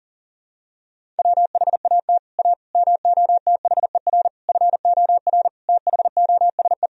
Morse code, heard as a keyed tone, about 40 wpm, 700 Hz. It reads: WHAT A MOTHER FOR THOSE
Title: Sentences from Top 1000 Words - Review Part 1 - 40wpm